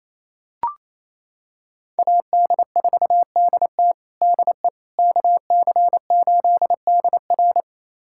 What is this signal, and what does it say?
Morse code, heard as a keyed tone, about 27 wpm, 690 Hz. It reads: AD4DT DE KC8DR
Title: USA Callsign DE Callsign - 5 Characters - 28wpm